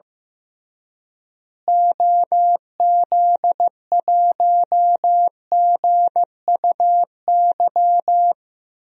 Morse code, heard as a keyed tone, about 15 wpm, 700 Hz. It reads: OZ1GUY
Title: DXCC-100 Call Signs - 6 Characters - Repeated 3x - 15wpm